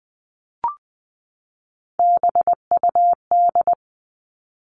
Morse code, heard as a keyed tone, about 20 wpm, 700 Hz. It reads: BUD
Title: All Cut Numbers - 3-Digit 20wpm